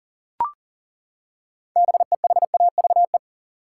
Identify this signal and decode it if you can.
Morse code, heard as a keyed tone, about 40 wpm, 700 Hz. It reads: BEHAVE
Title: Six Letter Words - 40wpm